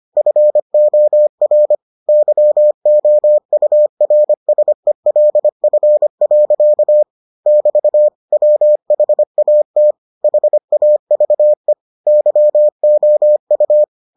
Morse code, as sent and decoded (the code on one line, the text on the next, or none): ..-. --- .-. / -.-- --- ..- .-. ... .-.. ..-. .-.-.- / -...- / .-- .... .- - / .... .- ...- . / -.-- --- ..-
FOR YOURSELF. <BT> WHAT HAVE YOU